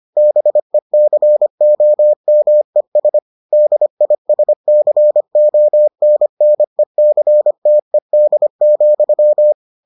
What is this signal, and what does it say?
Morse code, heard as a keyed tone, about 25 wpm, 600 Hz. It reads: BECOMES DISCONNECTED,